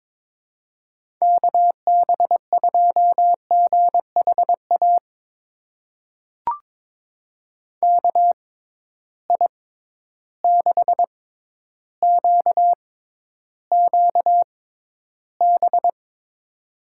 Morse code, heard as a keyed tone, about 22 wpm, spaced as for 11 wpm, 700 Hz. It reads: KB2GHA KI6QQB
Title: USA Call Signs - 6 Characters - 6x Character Spacing - 22wpm